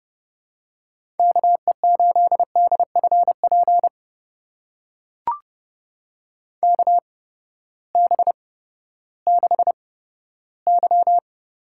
Morse code, heard as a keyed tone, about 31 wpm, spaced as for 15 wpm, 700 Hz. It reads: KE8DFP KB6Y
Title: USA Call Signs - 6 Characters - 8x Character Spacing - 30wpm